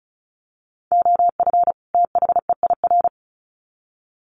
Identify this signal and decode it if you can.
Morse code, heard as a keyed tone, about 34 wpm, 700 Hz. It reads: OF THEIR
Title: Sets of 2 Words 35wpm